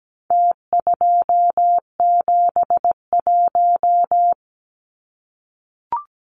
Morse code, heard as a keyed tone, about 17 wpm, 700 Hz. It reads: T271
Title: Numbers - Contest Style - 4-Digit 17wpm